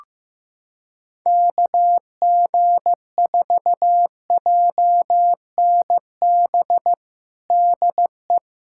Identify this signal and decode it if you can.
Morse code, heard as a keyed tone, about 15 wpm, 700 Hz. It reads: KG4JNB DE
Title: USA Callsign DE Callsign 15wpm